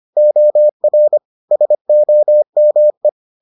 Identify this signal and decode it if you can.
Morse code, heard as a keyed tone, about 25 wpm, 600 Hz. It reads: OR SOME